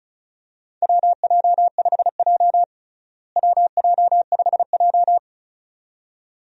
Morse code, heard as a keyed tone, about 34 wpm, 700 Hz. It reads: WJ5J WJ5J
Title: USA Call Signs - 4 Characters - Repeated 2x - 35wpm